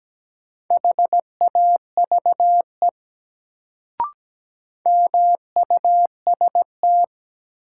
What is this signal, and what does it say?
Morse code, heard as a keyed tone, about 17 wpm, 700 Hz. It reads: HAVE MUST
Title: Sets of 2 Words - New Word Drill - 17wpm